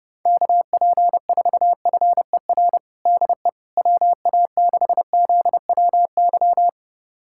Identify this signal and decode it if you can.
Morse code, heard as a keyed tone, about 30 wpm, 700 Hz. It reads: KP4FER DE WA6ZWY
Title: USA Callsign DE Callsign - 6 Characters - 30wpm